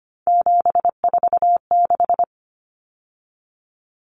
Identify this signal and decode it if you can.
Morse code, heard as a keyed tone, about 25 wpm, 700 Hz. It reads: 746
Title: Common Cut Numbers - 3-Digit 25wpm